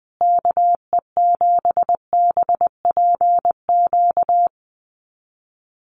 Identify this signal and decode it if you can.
Morse code, heard as a keyed tone, about 20 wpm, 700 Hz. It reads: KE7BPQ
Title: USA Call Signs Repeated 2x 20wpm